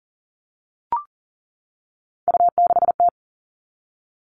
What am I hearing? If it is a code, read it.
Morse code, heard as a keyed tone, about 40 wpm, 700 Hz. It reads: U6T
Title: All Cut Numbers - 3-Digit 40wpm